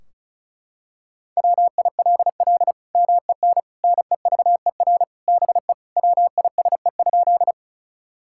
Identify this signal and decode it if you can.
Morse code, heard as a keyed tone, about 36 wpm, 700 Hz. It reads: WILL MEN NEVER BE WISE?
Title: Sentences from Top 1600 Words 35wpm